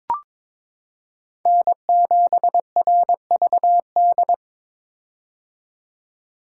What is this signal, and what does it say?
Morse code, heard as a keyed tone, about 22 wpm, 700 Hz. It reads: N7RVD
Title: K1USN SST QSO Practice - 22wpm